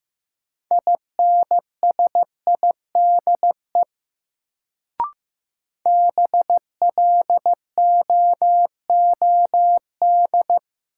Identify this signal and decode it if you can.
Morse code, heard as a keyed tone, about 15 wpm, 700 Hz. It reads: INSIDE BLOOD
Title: Sentences from Top 1000 Words - Review Part 2 - New Word Drill - 15wpm